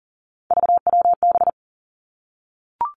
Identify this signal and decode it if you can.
Morse code, heard as a keyed tone, about 38 wpm, 700 Hz. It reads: VWB
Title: Three Challenging Letter-Number Combinations 40wpm